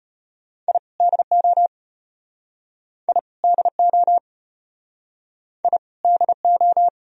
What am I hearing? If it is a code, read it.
Morse code, heard as a keyed tone, about 33 wpm, 700 Hz. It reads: I DO I DO I DO